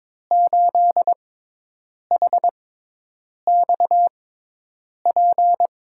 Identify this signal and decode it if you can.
Morse code, heard as a keyed tone, about 22 wpm, spaced as for 8 wpm, 700 Hz. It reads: 8HXP
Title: USA Call Signs - 6 Characters - 6x Character Spacing - 22wpm